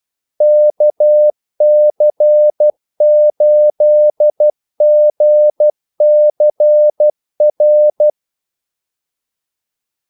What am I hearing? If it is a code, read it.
Morse code, heard as a keyed tone, about 12 wpm, 600 Hz. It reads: KC8GCR
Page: 2 ekstra ordmellemrum mellem kaldesignalerne